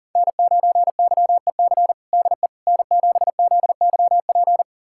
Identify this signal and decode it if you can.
Morse code, heard as a keyed tone, about 40 wpm, 700 Hz. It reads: N9YEC DE N7ZYP
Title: USA Callsign DE Callsign - 5 Characters - 40wpm